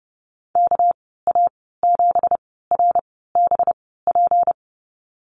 Morse code, heard as a keyed tone, about 30 wpm, spaced as for 17 wpm, 700 Hz. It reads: KA7RBP
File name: Lesson 37g - Letter x - Call Signs 30wpm